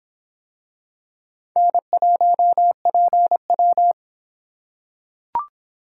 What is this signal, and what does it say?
Morse code, heard as a keyed tone, about 23 wpm, 700 Hz. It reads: N1PW